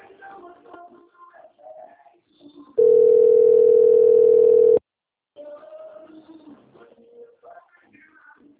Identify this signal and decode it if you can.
Ringback tone (North America), 440 + 480 Hz.
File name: rg-1600-unknown- - - .7503